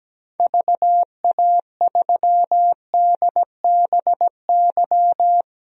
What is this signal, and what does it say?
Morse code, heard as a keyed tone, about 17 wpm, 700 Hz. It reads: VA3DBY
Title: DXCC-100 Call Signs - 6 Characters - 17wpm